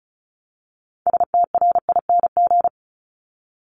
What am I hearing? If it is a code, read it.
Morse code, heard as a keyed tone, about 36 wpm, 690 Hz. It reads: STRING
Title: Multi-letter Phonemes 35wpm